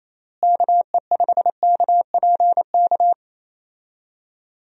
Morse code, heard as a keyed tone, about 28 wpm, 700 Hz. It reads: KE5KPK
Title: USA Call Signs - 6 Characters - Repeated 2x - 28wpm